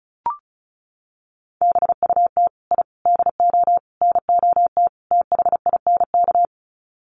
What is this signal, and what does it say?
Morse code, heard as a keyed tone, about 34 wpm, 700 Hz. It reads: BUT I DO NOT THINK
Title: Sets of 5 Words 35wpm